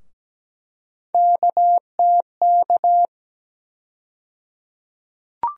Morse code, heard as a keyed tone, about 17 wpm, 700 Hz. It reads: KTK
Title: USA Call Sign Postfixes - 3 Letters 17wpm